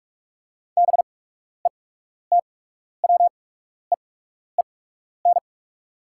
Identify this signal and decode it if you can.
Morse code, heard as a keyed tone, about 44 wpm, spaced as for 12 wpm, 690 Hz. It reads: BETWEEN
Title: 8x Character Spacing - Top 300 Words - 45wpm